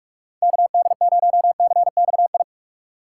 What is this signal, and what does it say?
Morse code, heard as a keyed tone, about 42 wpm, 700 Hz. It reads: KD0CXI
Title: USA Call Signs - 6x Character Spacing - 45wpm